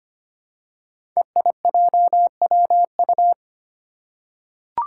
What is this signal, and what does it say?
Morse code, heard as a keyed tone, about 25 wpm, 700 Hz. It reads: EIJWU
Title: Five Challenging Letter-Number Combinations 25wpm